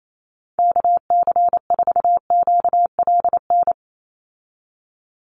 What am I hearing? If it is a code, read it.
Morse code, heard as a keyed tone, about 28 wpm, 700 Hz. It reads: KC4QLN
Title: USA Call Signs 28wpm